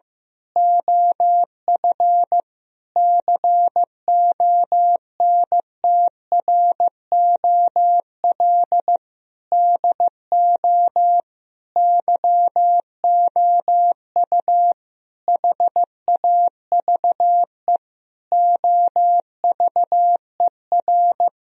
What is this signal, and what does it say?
Morse code, heard as a keyed tone, about 15 wpm, 700 Hz. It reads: OF CONTROL DO YOU HAVE OVER
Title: Sentences from Top 1000 Words - Review Part 3 - 15wpm